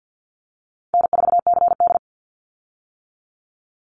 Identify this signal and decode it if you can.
Morse code, heard as a keyed tone, about 50 wpm, 700 Hz. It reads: N4CD
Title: Easy Call Signs - 50wpm